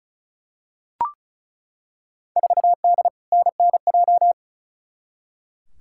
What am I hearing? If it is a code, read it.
Morse code, heard as a keyed tone, about 34 wpm, 700 Hz. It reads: 4D NNJ
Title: Field Day Exchanges Encore - 35wpm